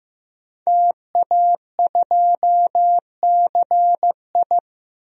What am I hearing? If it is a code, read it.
Morse code, heard as a keyed tone, about 16 wpm, 700 Hz. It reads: TA2CI
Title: DXCC-100 Call Signs - 5 Characters - Repeated 2x - 15wpm